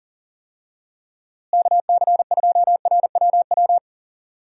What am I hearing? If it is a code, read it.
Morse code, heard as a keyed tone, about 40 wpm, 700 Hz. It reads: KC2RWW
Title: USA Call Signs - 6 Characters - Repeated 2x - 40wpm